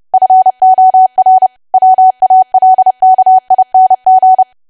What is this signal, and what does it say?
Morse code, heard as a keyed tone, about 30 wpm, 750 Hz. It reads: FOR WALKING
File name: cwa / W1AW-Level3 / 30WPM